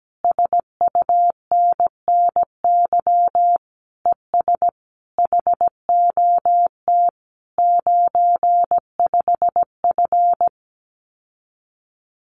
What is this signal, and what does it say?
Morse code, heard as a keyed tone, about 17 wpm, 700 Hz. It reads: SUNNY ES HOT 95F
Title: Common QSO Phrases Encore 17wpm